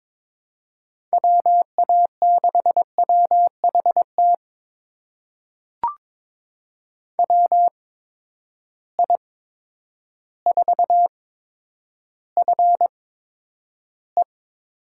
Morse code, heard as a keyed tone, about 22 wpm, spaced as for 10 wpm, 700 Hz. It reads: WA6WHT WI4FE
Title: USA Call Signs - 6 Characters - 8x Character Spacing - 22wpm